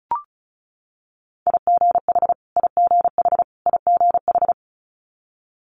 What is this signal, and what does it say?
Morse code, heard as a keyed tone, about 36 wpm, 700 Hz. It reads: IGH IGH IGH
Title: Multi-letter Phonemes Repeated 3x 35wpm